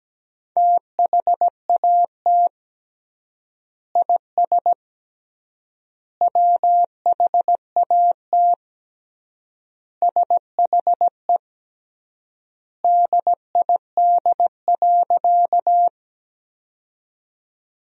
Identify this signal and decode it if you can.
Morse code, heard as a keyed tone, about 17 wpm, 700 Hz. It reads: THAT IS WHAT SHE DID.
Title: 3x Word-Spacing - Sentences from Top 100 Words - 17wpm